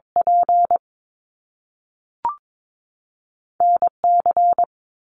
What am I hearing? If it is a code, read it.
Morse code, heard as a keyed tone, about 22 wpm, 700 Hz. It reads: P N C
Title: Top Two Letter Combinations in Words 22wpm